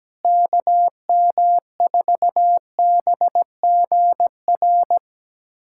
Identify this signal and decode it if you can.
Morse code, heard as a keyed tone, about 17 wpm, 700 Hz. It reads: KM4BGR
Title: USA Callsign DE Callsign 17wpm